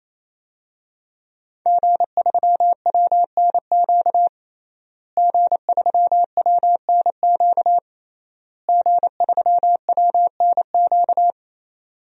Morse code, heard as a keyed tone, about 28 wpm, 700 Hz. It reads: G3WNQ G3WNQ G3WNQ
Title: DXCC-100 Call Signs - 5 Characters - Repeated 3x - 28wpm